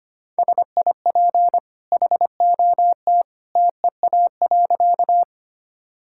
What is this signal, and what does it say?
Morse code, heard as a keyed tone, about 25 wpm, 700 Hz. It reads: SIP HOT TEA.
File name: Easy Sentences with Hints - 25wpm